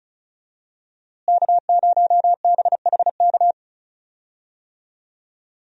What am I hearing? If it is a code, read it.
Morse code, heard as a keyed tone, about 36 wpm, 700 Hz. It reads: K0BHK
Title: USA Call Signs - 5 Characters - Repeated 3x - 35wpm